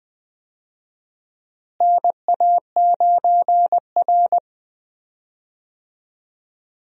Morse code, heard as a keyed tone, about 20 wpm, 700 Hz. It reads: NA9R
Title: USA Call Signs - 4 Characters - Repeated 2x - 20wpm